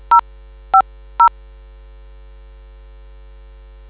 DTMF, heard as telephone touch tones, tones of about 100 ms. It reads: 050